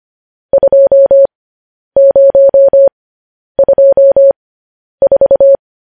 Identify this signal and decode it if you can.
Morse code, heard as a keyed tone, about 25 wpm, spaced as for 10 wpm, 550 Hz. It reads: 2024
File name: podcast / News headlines-1x-25x10